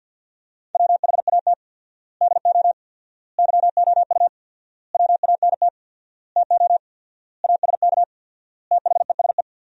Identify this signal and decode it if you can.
Morse code, heard as a keyed tone, about 51 wpm, 700 Hz. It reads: WHAT DO YOU WANT TO ASK THESE
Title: 4x Word-Spacing - Sentences from Top 200 Words - 50wpm